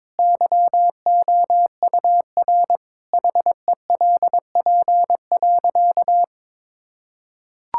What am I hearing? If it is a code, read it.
Morse code, heard as a keyed tone, about 22 wpm, 700 Hz. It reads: YOUR HELP.
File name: practice / 22wpm / Sentences with 4-Word Phrases - 22wpm